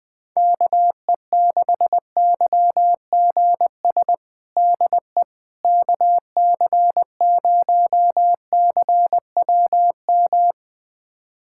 Morse code, heard as a keyed tone, about 20 wpm, 700 Hz. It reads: KE6YGS DE KC0CWM
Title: USA Callsign DE Callsign 20wpm